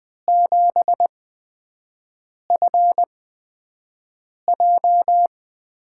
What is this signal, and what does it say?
Morse code, heard as a keyed tone, about 20 wpm, spaced as for 5 wpm, 700 Hz. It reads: 7FJ
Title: USA Call Signs - 6 Characters - 8x Character Spacing - 20wpm